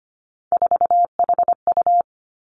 Morse code, heard as a keyed tone, about 25 wpm, 700 Hz. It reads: 4HU
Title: Three Challenging Letter-Number Combinations 25wpm